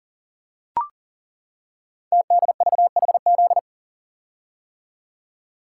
Morse code, heard as a keyed tone, about 38 wpm, 700 Hz. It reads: TDVHZ
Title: Code Groups - 5 Letters - 40wpm